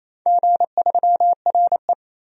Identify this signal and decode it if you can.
Morse code, heard as a keyed tone, about 28 wpm, 700 Hz. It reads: G3RE
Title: DXCC-100 Call Signs - 4 Characters - 28wpm